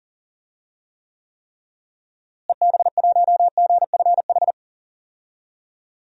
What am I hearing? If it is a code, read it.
Morse code, heard as a keyed tone, about 37 wpm, 700 Hz. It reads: EB1GFH
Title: DXCC-100 Call Signs - 6 Characters - Repeated 3x - 40wpm